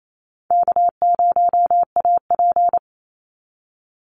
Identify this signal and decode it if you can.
Morse code, heard as a keyed tone, about 28 wpm, 700 Hz. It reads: K0AP
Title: Easy Call Signs Repeated 2x - 28wpm